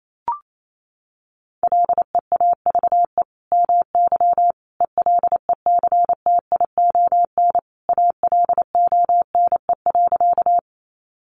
Morse code, heard as a keyed tone, about 28 wpm, 700 Hz. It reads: LEAVE MY ELECTION ALONE.
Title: Sentences from Top 1500 Words 28wpm